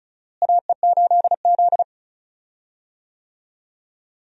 Morse code, heard as a keyed tone, about 36 wpm, 700 Hz. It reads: AE8Z
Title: USA Call Signs - 4 Characters - Repeated 3x - 35wpm